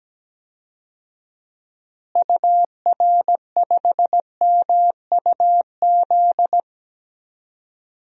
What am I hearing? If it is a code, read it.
Morse code, heard as a keyed tone, about 17 wpm, 700 Hz. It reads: UR5MUZ
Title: DXCC-100 Call Signs - 6 Characters - Repeated 2x - 17wpm